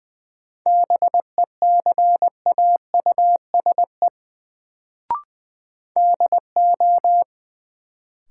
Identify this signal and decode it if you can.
Morse code, heard as a keyed tone, about 20 wpm, 700 Hz. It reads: BECAUSE DO
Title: Sets of 2 Words - New Word Drill - 20wpm